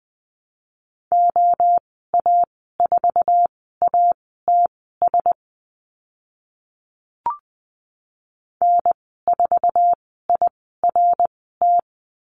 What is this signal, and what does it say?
Morse code, heard as a keyed tone, about 20 wpm, spaced as for 14 wpm, 700 Hz. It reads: OA4ATS N4IRT
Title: 082 Lesson 14g - Number 2 - Call Signs 20wpm